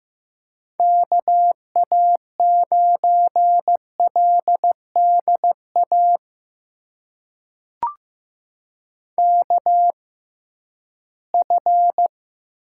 Morse code, heard as a keyed tone, about 15 wpm, spaced as for 11 wpm, 700 Hz. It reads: KA9LDA KF
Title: USA Call Signs - 6 Characters - 6x Character Spacing - 15wpm